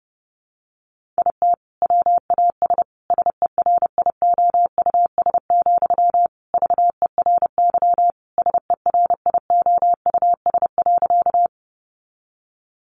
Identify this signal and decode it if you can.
Morse code, heard as a keyed tone, about 30 wpm, 700 Hz. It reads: IT WAS SERIOUS, VERY SERIOUS.
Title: Sentences from Top 1100 Words 30wpm